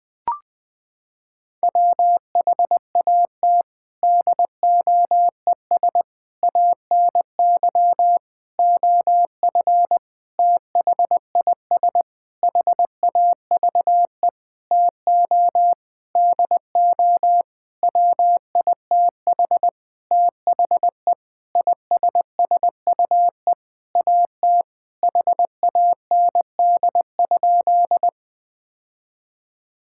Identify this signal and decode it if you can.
Morse code, heard as a keyed tone, about 20 wpm, 700 Hz. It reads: WHAT DOES ANY OF THIS HAVE TO DO WITH THE ISSUE AT HAND?
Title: Sentences with 9-Word Phrases - 20wpm